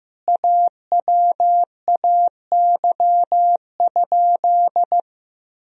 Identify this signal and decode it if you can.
Morse code, heard as a keyed tone, about 15 wpm, 700 Hz. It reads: AWAY?
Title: Sentences from Top 1000 Words - Review Part 1 - 15wpm